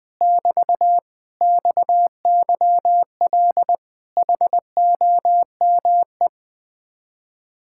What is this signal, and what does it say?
Morse code, heard as a keyed tone, about 20 wpm, 700 Hz. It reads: <BT> XYL HOME
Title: Speed-Racing Common QSO Phrases - 17wpm